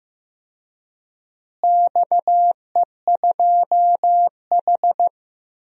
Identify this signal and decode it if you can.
Morse code, heard as a keyed tone, about 15 wpm, 700 Hz. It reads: XE2H
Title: DXCC-100 Call Signs - 4 Characters - Repeated 2x - 15wpm